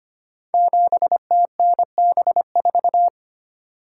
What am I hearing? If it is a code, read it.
Morse code, heard as a keyed tone, about 25 wpm, 700 Hz. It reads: 7TNB4
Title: Code Groups - 5 Letter:Numbers - 25wpm